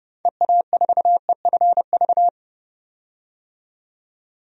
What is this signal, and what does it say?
Morse code, heard as a keyed tone, about 31 wpm, 700 Hz. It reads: EA4EFV
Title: DXCC-100 Call Signs - 6 Characters - 30wpm